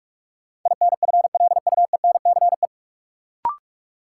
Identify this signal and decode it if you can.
Morse code, heard as a keyed tone, about 46 wpm, 700 Hz. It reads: INFLUENCE
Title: Sentences from Top 1100 Words - New Word Drill - 45wpm